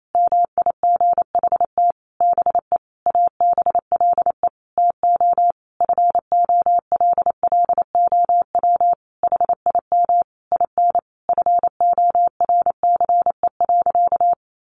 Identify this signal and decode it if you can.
Morse code, heard as a keyed tone, about 28 wpm, 700 Hz. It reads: MIGHT BE ABLE TO FOLLOW HIM IN FORCE.
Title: Sentences from Top 700 Words 28wpm